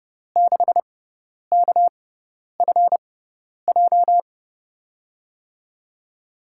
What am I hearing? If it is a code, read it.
Morse code, heard as a keyed tone, about 30 wpm, spaced as for 10 wpm, 700 Hz. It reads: 6KFJ
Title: USA Call Signs - 5 Characters - 6x Character Spacing - 30wpm